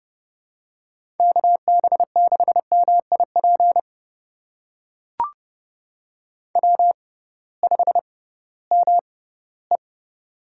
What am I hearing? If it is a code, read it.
Morse code, heard as a keyed tone, about 31 wpm, spaced as for 18 wpm, 700 Hz. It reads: KB6MIP W5ME
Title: USA Call Signs - 6x Character Spacing - 30wpm